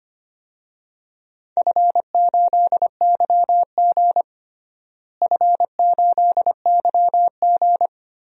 Morse code, heard as a keyed tone, about 25 wpm, 700 Hz. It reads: F8YG F8YG
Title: DXCC-100 Call Signs - 4 Characters - Repeated 3x - 25wpm